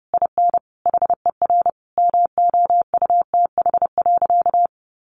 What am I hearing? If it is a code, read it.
Morse code, heard as a keyed tone, about 30 wpm, 700 Hz. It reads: IN HER MOUTH.
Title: Sentences from Top 1500 Words 30wpm